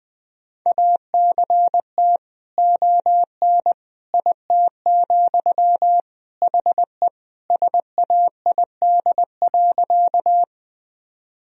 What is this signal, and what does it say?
Morse code, heard as a keyed tone, about 20 wpm, 700 Hz. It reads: ACT ON IT, HE SAID.
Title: Sentences from Top 1000 Words - Review Part 1 - 20wpm